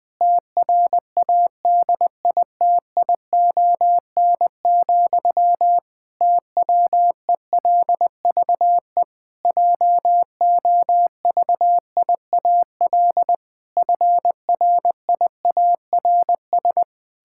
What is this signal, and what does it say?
Morse code, heard as a keyed tone, about 20 wpm, 700 Hz. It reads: TRADITION, TWELVE JOVIAL FRIARS
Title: Pangram Challenge v2 - 20wpm